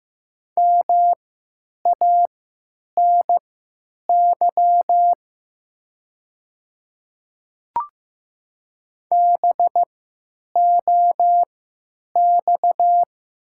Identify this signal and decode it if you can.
Morse code, heard as a keyed tone, about 15 wpm, spaced as for 9 wpm, 700 Hz. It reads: MANY BOX
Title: 3x Character Spacing - Top 500 Words - 15wpm